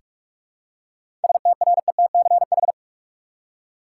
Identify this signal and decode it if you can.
Morse code, heard as a keyed tone, about 42 wpm, 700 Hz. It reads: STRETCH
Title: Seven Letter Words - 45wpm